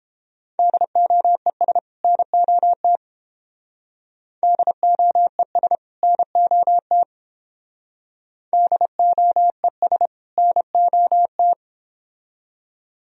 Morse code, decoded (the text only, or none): DOES NOT DOES NOT DOES NOT